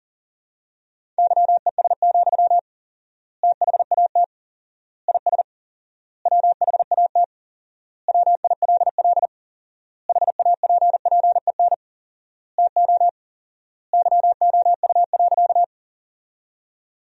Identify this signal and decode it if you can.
Morse code, heard as a keyed tone, about 40 wpm, 700 Hz. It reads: YES, THAT IS WHAT WILL HAPPEN TO YOU.
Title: 4x Word-Spacing - Sentences from Top 500 Words - 40wpm